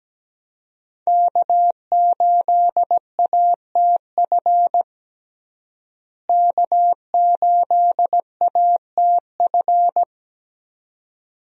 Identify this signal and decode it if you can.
Morse code, heard as a keyed tone, about 17 wpm, 700 Hz. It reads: K8ATF K8ATF
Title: USA Call Signs - 5 Characters - Repeated 2x - 17wpm